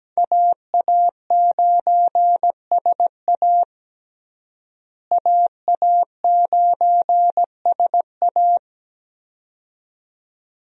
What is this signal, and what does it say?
Morse code, heard as a keyed tone, about 17 wpm, 700 Hz. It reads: AA9SA AA9SA
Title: USA Call Signs - 5 Characters - Repeated 2x - 17wpm